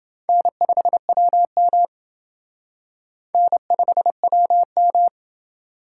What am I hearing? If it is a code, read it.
Morse code, heard as a keyed tone, about 28 wpm, 700 Hz. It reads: N5WM N5WM